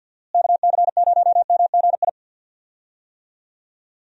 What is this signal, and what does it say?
Morse code, heard as a keyed tone, about 50 wpm, 700 Hz. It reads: KC0MGI
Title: USA Call Signs - 6 Characters - Repeated 3x - 50wpm